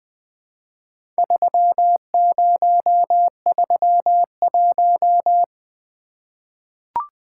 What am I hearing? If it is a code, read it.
Morse code, heard as a keyed tone, about 20 wpm, 700 Hz. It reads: 3031
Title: Four Letter-Number Combinations - 20wpm